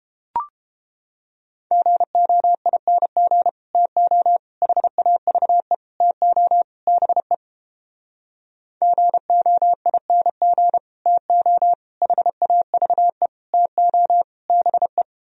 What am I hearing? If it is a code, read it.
Morse code, heard as a keyed tone, about 29 wpm, 700 Hz. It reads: GOING TO HAVE TO BE GOING TO HAVE TO BE